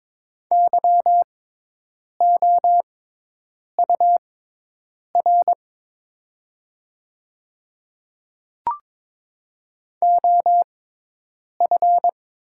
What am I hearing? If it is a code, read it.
Morse code, heard as a keyed tone, about 22 wpm, spaced as for 8 wpm, 700 Hz. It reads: YOUR OF
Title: 6x Character Spacing - Top 100 Words - 22wpm